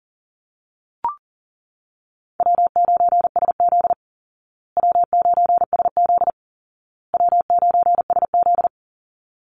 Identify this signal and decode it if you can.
Morse code, heard as a keyed tone, about 40 wpm, 700 Hz. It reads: W9SZ W9SZ W9SZ
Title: Easy Call Signs Repeated 3x - 40wpm